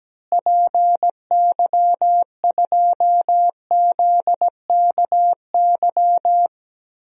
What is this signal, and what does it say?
Morse code, heard as a keyed tone, about 17 wpm, 700 Hz. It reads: PY2ZKY
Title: DXCC-100 Call Signs - 6 Characters - Repeated 3x - 17wpm